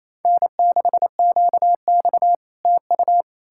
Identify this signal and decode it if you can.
Morse code, heard as a keyed tone, about 38 wpm, 700 Hz. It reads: N6QX TU